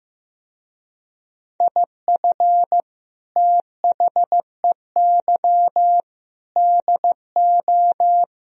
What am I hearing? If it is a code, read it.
Morse code, heard as a keyed tone, about 16 wpm, 700 Hz. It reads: IF THEY DO
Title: Speed-Racing - Sets of 4 Words 15wpm